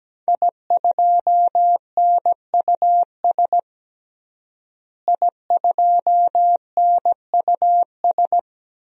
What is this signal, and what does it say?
Morse code, heard as a keyed tone, about 17 wpm, 700 Hz. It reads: I2NUS I2NUS
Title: DXCC-100 Call Signs - 5 Characters - Repeated 2x - 17wpm